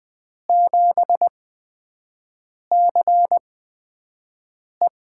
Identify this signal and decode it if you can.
Morse code, heard as a keyed tone, about 20 wpm, spaced as for 5 wpm, 700 Hz. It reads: 7CE
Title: USA Call Signs - 5 Characters - 8x Character Spacing - 20wpm